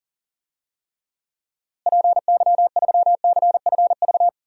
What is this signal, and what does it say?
Morse code, heard as a keyed tone, about 40 wpm, 700 Hz. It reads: PY3CFV
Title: DXCC-100 Call Signs - 6 Characters - Repeated 2x - 40wpm